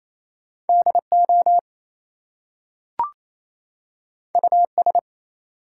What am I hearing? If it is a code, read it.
Morse code, heard as a keyed tone, about 29 wpm, 700 Hz. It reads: DO US
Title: Two Letter Words - 28wpm